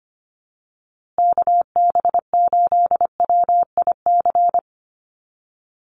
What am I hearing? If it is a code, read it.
Morse code, heard as a keyed tone, about 25 wpm, 700 Hz. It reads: KB8WIC
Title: USA Call Signs Repeated 2x 25wpm